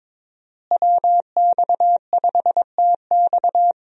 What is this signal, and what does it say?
Morse code, heard as a keyed tone, about 22 wpm, 700 Hz. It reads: WX5TX
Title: USA Call Signs - 5 Characters - Repeated 3x - 22wpm